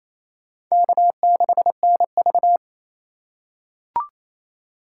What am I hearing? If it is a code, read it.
Morse code, heard as a keyed tone, about 28 wpm, 700 Hz. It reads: K6NV
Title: USA Call Signs - 4 Characters - 2x Character Spacing - 28wpm